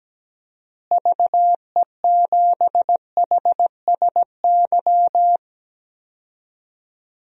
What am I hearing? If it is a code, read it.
Morse code, heard as a keyed tone, about 17 wpm, 700 Hz. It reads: VE7HSY
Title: DXCC-100 Call Signs - 6 Characters - Repeated 2x - 17wpm